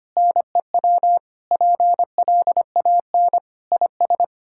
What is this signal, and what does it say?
Morse code, heard as a keyed tone, about 25 wpm, 700 Hz. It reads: NEW PLAN IS
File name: practice / 25wpm / Sentences with 2-Word Phrases - 25wpm